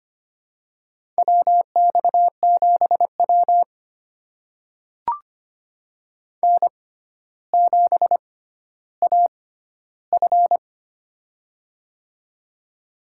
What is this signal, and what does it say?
Morse code, heard as a keyed tone, about 25 wpm, spaced as for 13 wpm, 700 Hz. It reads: WX7W N7AF
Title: USA Call Signs - 4 Characters - 6x Character Spacing - 25wpm